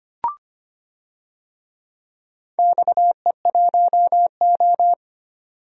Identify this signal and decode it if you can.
Morse code, heard as a keyed tone, about 25 wpm, 700 Hz. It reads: XE1O
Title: DXCC-100 Call Signs - 4 Characters - 25wpm